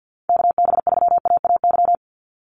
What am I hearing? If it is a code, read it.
Morse code, heard as a keyed tone, about 50 wpm, 700 Hz. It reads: KB3AAY
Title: Call Signs - 50wpm